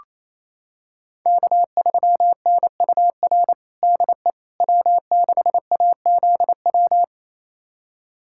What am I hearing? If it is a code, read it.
Morse code, heard as a keyed tone, about 28 wpm, 700 Hz. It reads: K3NUR DE W6AZW
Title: USA Callsign DE Callsign - 5 Characters - 28wpm